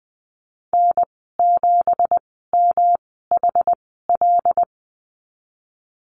Morse code, heard as a keyed tone, about 20 wpm, spaced as for 15 wpm, 700 Hz. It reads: N7MHL
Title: 200 Lesson 32g - Letter k - Call Signs 20wpm